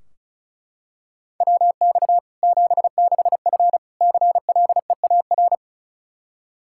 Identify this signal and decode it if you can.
Morse code, heard as a keyed tone, about 36 wpm, 700 Hz. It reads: WX 76F CLEAR
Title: K1USN SST QSO Practice 35wpm